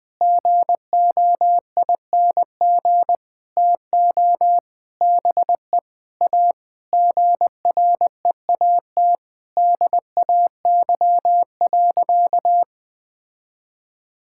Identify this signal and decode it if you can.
Morse code, heard as a keyed tone, about 20 wpm, 700 Hz. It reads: GOING TO BE A GREAT DAY.
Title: Sentences with 6-Word Phrases - Spoken Beginning - 20wpm